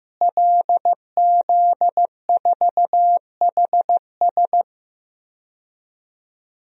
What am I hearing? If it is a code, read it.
Morse code, heard as a keyed tone, about 15 wpm, 700 Hz. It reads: LZ4HS
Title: DXCC-100 Call Signs - 5 Characters - Repeated 3x - 15wpm